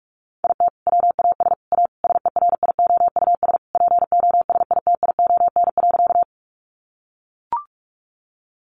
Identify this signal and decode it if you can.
Morse code, heard as a keyed tone, about 44 wpm, 700 Hz. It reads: IT WAS A SERIOUS POSITION.
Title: Sentences from Top 1100 Words 45wpm